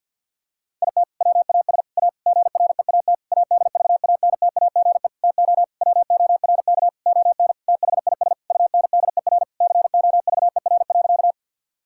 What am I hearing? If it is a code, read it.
Morse code, heard as a keyed tone, about 50 wpm, 700 Hz. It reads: IT WAS A GREAT ADVANTAGE TO WORK ON THIS UNDER COVER.
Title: Sentences from Top 1000 Words - Review Part 6 - 50wpm